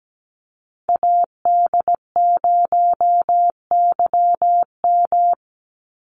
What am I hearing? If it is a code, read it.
Morse code, heard as a keyed tone, about 17 wpm, 700 Hz. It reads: AD0YM
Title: NA SOTA Chaser Callsigns 17wpm